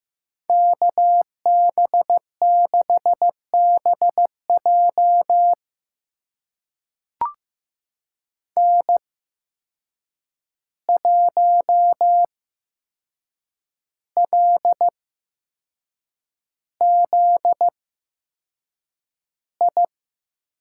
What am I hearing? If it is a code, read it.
Morse code, heard as a keyed tone, about 15 wpm, spaced as for 6 wpm, 700 Hz. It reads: KB6BJ N1LZI
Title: USA Call Signs - 5 Characters - 8x Character Spacing - 15wpm